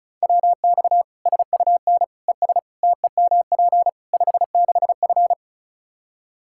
Morse code, heard as a keyed tone, about 34 wpm, 700 Hz. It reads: WX SUN ES TEMP 56F
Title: Practice QSO - Exchange 2 - Info - 35wpm